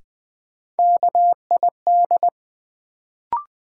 Morse code, heard as a keyed tone, about 20 wpm, 700 Hz. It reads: KID
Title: Sets of 5 Words - New Word Drill - 20wpm